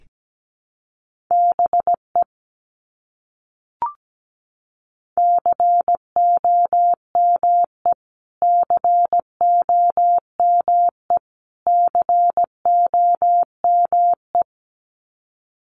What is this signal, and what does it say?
Morse code, heard as a keyed tone, about 17 wpm, 700 Hz. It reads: BE COME COME COME
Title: Top 200 Words Repeated 17wpm